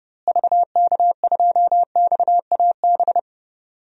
Morse code, heard as a keyed tone, about 30 wpm, 700 Hz. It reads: VK2XAB
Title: DXCC-100 Call Signs - 6 Characters - Repeated 2x - 30wpm